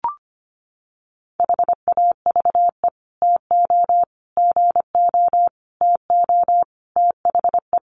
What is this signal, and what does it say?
Morse code, heard as a keyed tone, about 25 wpm, 700 Hz. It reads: HAVE TO GO TO THE
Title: Sets of 5 Words Encore 25wpm